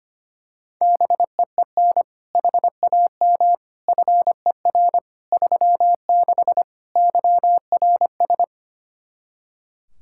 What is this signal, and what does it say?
Morse code, heard as a keyed tone, about 25 wpm, 700 Hz. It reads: BEEN HAM FER 36 YRS
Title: Practice QSO - Exchange 3 - More Info - 25wpm